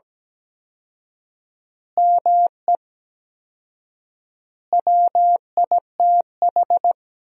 Morse code, heard as a keyed tone, about 17 wpm, 700 Hz. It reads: ME WITH
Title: 4x Word-Spacing - Sentences from Top 500 Words - 17wpm